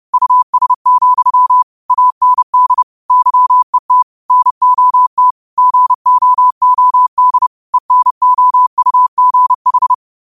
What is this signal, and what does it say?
Morse code, heard as a keyed tone, about 30 wpm, 1000 Hz. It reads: US, AND YET NOT GOOD ENOUGH
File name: qotdfiles / 30wpm